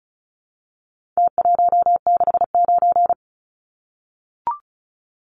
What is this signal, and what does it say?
Morse code, heard as a keyed tone, about 34 wpm, 700 Hz. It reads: T169
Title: Numbers - Contest Style - 4-Digit 35wpm